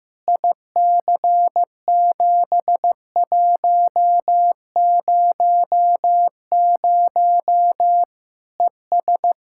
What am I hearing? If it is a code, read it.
Morse code, heard as a keyed tone, about 15 wpm, 700 Hz. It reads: IC7100 ES
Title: Practice QSO - Exchange 2 - Info - 15wpm